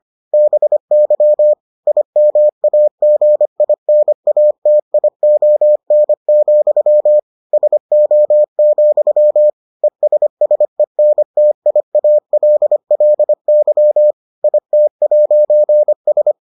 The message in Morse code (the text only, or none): BY IMAGINATION, SO, ESSENTIALLY IT'S